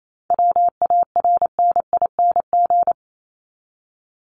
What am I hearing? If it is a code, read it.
Morse code, heard as a keyed tone, about 28 wpm, 700 Hz. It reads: WARNING
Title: Top 1751-2000 Words 28wpm